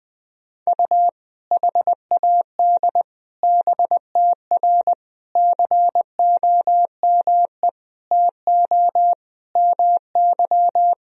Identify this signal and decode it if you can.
Morse code, heard as a keyed tone, about 20 wpm, 700 Hz. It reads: U HAD BTR COME TO MY
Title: Sentences from Top 400 Words and Abbreviations 20wpm